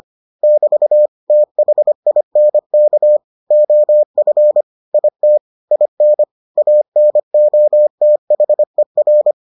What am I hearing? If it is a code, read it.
Morse code, heard as a keyed tone, about 25 wpm, 600 Hz. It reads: <BT> THINK OF IT IN ANOTHER